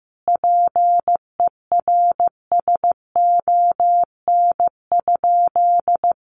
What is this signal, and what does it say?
Morse code, heard as a keyed tone, about 15 wpm, 700 Hz. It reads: PERSON?
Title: Sentences from Top 1400 Words 15wpm